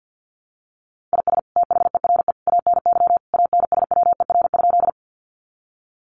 Morse code, heard as a keyed tone, about 50 wpm, 700 Hz. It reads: IS THERE ANY ANSWER?
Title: Sentences from Top 300 Words 50wpm